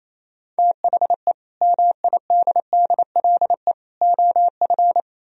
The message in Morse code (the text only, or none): THE MIDDLE OF